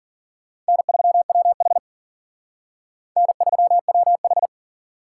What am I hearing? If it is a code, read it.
Morse code, heard as a keyed tone, about 47 wpm, 700 Hz. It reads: N3WH N3WH
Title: Speed Racing - USA 1x2 Call Signs 40wpm